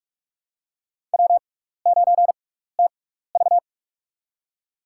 Morse code, heard as a keyed tone, about 44 wpm, spaced as for 15 wpm, 700 Hz. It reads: W9TV
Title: USA Call Signs - 4 Characters - 6x Character Spacing - 45wpm